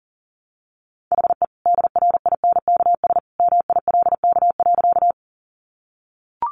Morse code, heard as a keyed tone, about 40 wpm, 700 Hz. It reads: HE DRINKS MILK.
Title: Easy Phrases 40wpm